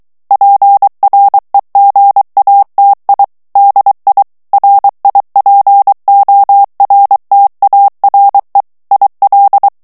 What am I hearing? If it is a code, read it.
Morse code, heard as a keyed tone, about 23 wpm, 800 Hz. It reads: PREGATI DI RIPORTARE IL
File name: ITALIEN texte clair 2 vitesse 22 mots minute